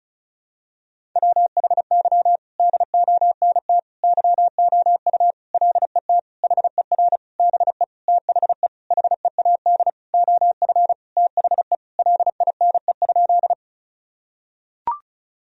Morse code, heard as a keyed tone, about 36 wpm, 700 Hz. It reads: WHY DONT YOU LET HER BE THE HEAD OF THE LINE?
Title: Sentences from Top 1000 Words - Review Part 1 - 35wpm